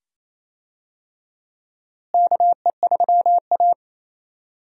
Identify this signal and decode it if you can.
Morse code, heard as a keyed tone, about 28 wpm, 700 Hz. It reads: KE3A
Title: USA Call Signs - 4 Characters - Repeated 2x - 28wpm